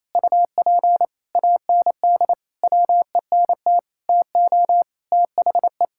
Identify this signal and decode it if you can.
Morse code, heard as a keyed tone, about 28 wpm, 700 Hz. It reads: UP AND WENT TO THE
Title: Sentences from Top 1600 Words 28wpm